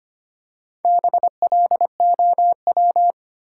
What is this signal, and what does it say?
Morse code, heard as a keyed tone, about 25 wpm, 700 Hz. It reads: BLOW
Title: Four Letter Words - 25wpm